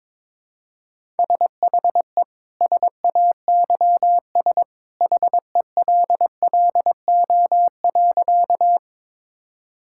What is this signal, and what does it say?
Morse code, heard as a keyed tone, about 22 wpm, 700 Hz. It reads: SHE SAYS HELLO.
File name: practice / 22wpm / Sentences with 2-Word Phrases - 22wpm